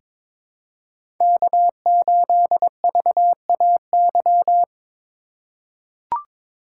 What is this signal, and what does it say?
Morse code, heard as a keyed tone, about 22 wpm, 700 Hz. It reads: K8VAY
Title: USA Call Signs - 5 Characters - 4x Character Spacing - 22wpm